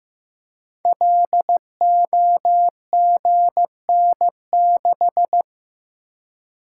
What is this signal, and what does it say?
Morse code, heard as a keyed tone, about 15 wpm, 700 Hz. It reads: LOGN6
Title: Code Groups - 5 Letter:Numbers - 15wpm